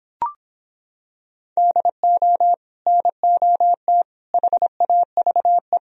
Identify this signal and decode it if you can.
Morse code, heard as a keyed tone, about 26 wpm, 700 Hz. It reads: DO NOT HAVE